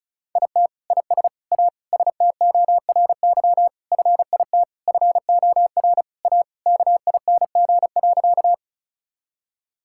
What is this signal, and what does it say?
Morse code, heard as a keyed tone, about 34 wpm, 700 Hz. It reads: IT IS A STORY FIT FOR A KING.
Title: Sentences from Top 1000 Words - Review Part 5 - 35wpm